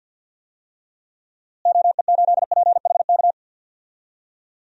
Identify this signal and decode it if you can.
Morse code, heard as a keyed tone, about 48 wpm, 700 Hz. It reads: OE8PHK
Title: DXCC-100 Call Signs - 6 Characters - Repeated 3x - 50wpm